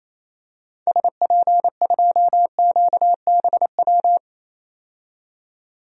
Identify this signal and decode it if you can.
Morse code, heard as a keyed tone, about 27 wpm, 700 Hz. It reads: SP2QBW
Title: DXCC-100 Call Signs - 6 Characters - 28wpm